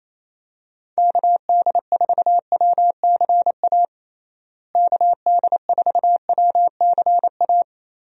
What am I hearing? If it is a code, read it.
Morse code, heard as a keyed tone, about 28 wpm, 700 Hz. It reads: KD4WCA KD4WCA
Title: USA Call Signs - 6 Characters - Repeated 2x - 28wpm